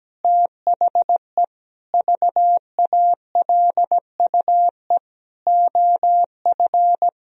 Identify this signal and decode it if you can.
Morse code, heard as a keyed tone, about 17 wpm, 700 Hz. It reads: THE VALUE OF
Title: Sentences from Top 1000 Words - Review Part 3 - 17wpm